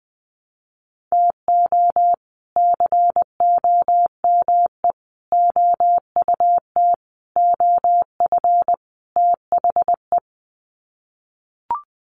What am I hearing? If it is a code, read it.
Morse code, heard as a keyed tone, about 19 wpm, 700 Hz. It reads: TO COME OUT OF THE
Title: Sets of 5 Words Encore2 20wpm